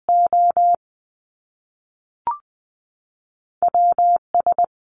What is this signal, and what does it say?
Morse code, heard as a keyed tone, about 19 wpm, spaced as for 3 wpm, 700 Hz. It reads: OW S
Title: Top Two Letter Combinations in Words 20wpm